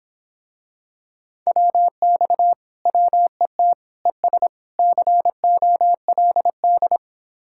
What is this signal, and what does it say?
Morse code, heard as a keyed tone, about 26 wpm, 700 Hz. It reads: WX WET ES COLD